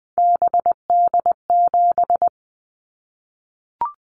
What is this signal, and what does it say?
Morse code, heard as a keyed tone, about 20 wpm, 700 Hz. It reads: BD7
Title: Callsign Prefixes for Top 100 DXCC - 20wpm